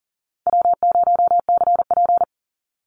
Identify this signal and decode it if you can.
Morse code, heard as a keyed tone, about 40 wpm, 700 Hz. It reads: W0CP
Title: Easy Call Signs - 40wpm